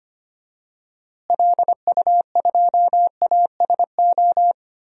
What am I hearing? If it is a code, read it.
Morse code, heard as a keyed tone, about 25 wpm, 700 Hz. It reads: LU2ASO
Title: DXCC-100 Call Signs - 6 Characters - Repeated 3x - 25wpm